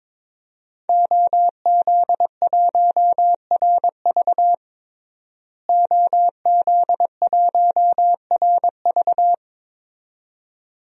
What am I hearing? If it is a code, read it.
Morse code, heard as a keyed tone, about 21 wpm, 700 Hz. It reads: OZ1RV OZ1RV
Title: DXCC-100 Call Signs - 5 Characters - Repeated 2x - 22wpm